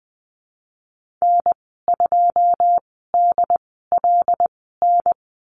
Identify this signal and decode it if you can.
Morse code, heard as a keyed tone, about 19 wpm, spaced as for 14 wpm, 700 Hz. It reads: N2DLN
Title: 154 Lesson 25g - Letter p - Call Signs 20wpm